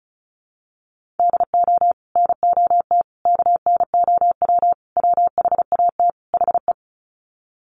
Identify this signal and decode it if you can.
Morse code, heard as a keyed tone, about 34 wpm, 700 Hz. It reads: DO NOT KNOW WHAT HE
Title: Sets of 5 Words Encore2 35wpm